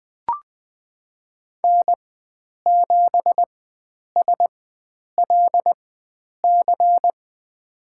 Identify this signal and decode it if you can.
Morse code, heard as a keyed tone, about 20 wpm, spaced as for 9 wpm, 700 Hz. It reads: N7SLC
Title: USA Call Signs - 5 Characters - 4x Character Spacing - 20wpm